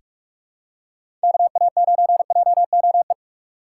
Morse code, heard as a keyed tone, about 46 wpm, 700 Hz. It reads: KA9JOE
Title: USA Call Signs - 6 Characters - 2x Character Spacing - 45wpm